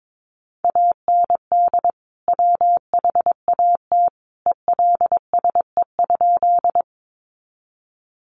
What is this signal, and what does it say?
Morse code, heard as a keyed tone, about 22 wpm, 700 Hz. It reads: AND WHAT ELSE?
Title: Sentences from Top 800 Words 22wpm